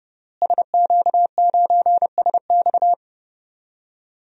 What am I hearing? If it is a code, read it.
Morse code, heard as a keyed tone, about 30 wpm, 700 Hz. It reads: SQ9SX
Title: DXCC-100 Call Signs - 5 Characters - Repeated 3x - 30wpm